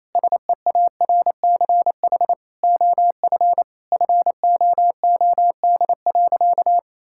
Morse code, heard as a keyed tone, about 27 wpm, 700 Hz. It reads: SEARCH OF FOOD.
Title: Sentences from Top 1000 Words - Review Part 5 - 28wpm